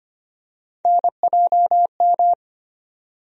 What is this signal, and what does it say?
Morse code, heard as a keyed tone, about 25 wpm, 700 Hz. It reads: NJM
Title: USA Call Sign Postfixes - 3 Letters 25wpm